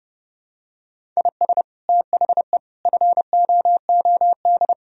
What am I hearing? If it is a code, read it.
Morse code, heard as a keyed tone, about 32 wpm, 700 Hz. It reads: IS THE FOOD
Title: Sentences from Top 1000 Words - Review Part 1 - 30wpm